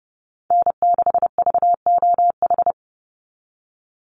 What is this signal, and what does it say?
Morse code, heard as a keyed tone, about 30 wpm, 700 Hz. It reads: N6VOH
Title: Call Signs - 30wpm